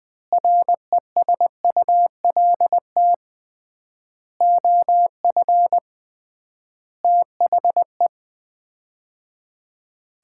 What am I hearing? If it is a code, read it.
Morse code, heard as a keyed tone, about 21 wpm, 700 Hz. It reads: RESULT OF THE
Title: 3x Word-Spacing - Sets of 5 Words - 20wpm